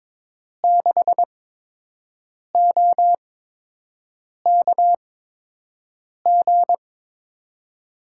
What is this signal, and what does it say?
Morse code, heard as a keyed tone, about 22 wpm, spaced as for 6 wpm, 700 Hz. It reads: 6OKG
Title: USA Call Signs - 8x Character Spacing - 22wpm